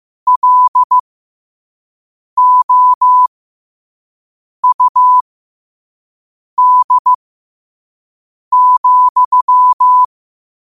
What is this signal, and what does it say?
Morse code, heard as a keyed tone, about 15 wpm, spaced as for 5 wpm, 1000 Hz. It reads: LOUD,
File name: qotdfiles / 5wpm